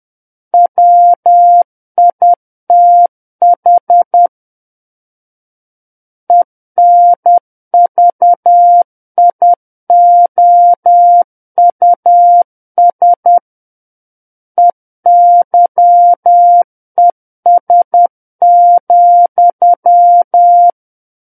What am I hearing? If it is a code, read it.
Morse code, heard as a keyed tone, about 10 wpm, 700 Hz. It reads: WITH ENVIOUS EYES,
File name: Book One / 01-Chapter 1 - 10 WPM